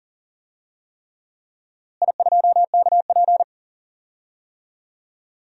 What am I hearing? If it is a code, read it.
Morse code, heard as a keyed tone, about 40 wpm, 700 Hz. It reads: I2KP
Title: DXCC-100 Call Signs - 4 Characters - Repeated 3x - 40wpm